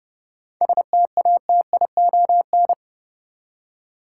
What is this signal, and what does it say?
Morse code, heard as a keyed tone, about 30 wpm, 700 Hz. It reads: STATION
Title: Seven Letter Words - 30wpm